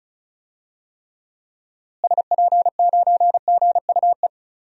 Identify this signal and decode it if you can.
Morse code, heard as a keyed tone, about 34 wpm, 700 Hz. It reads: SP9GUE
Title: DXCC-100 Call Signs - 6 Characters - Repeated 3x - 35wpm